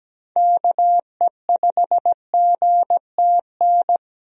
Morse code, heard as a keyed tone, about 17 wpm, 700 Hz. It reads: KE5GTN
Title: USA Call Signs - 6 Characters - Repeated 2x - 17wpm